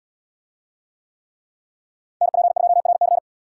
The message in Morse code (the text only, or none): NP2RC